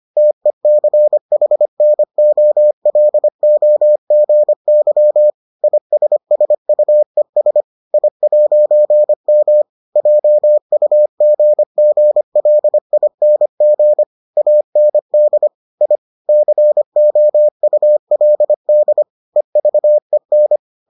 Morse code, heard as a keyed tone, about 25 wpm, 600 Hz. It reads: TECHNOLOGY ISSUES I'M JUGGLING AND I COULD EVEN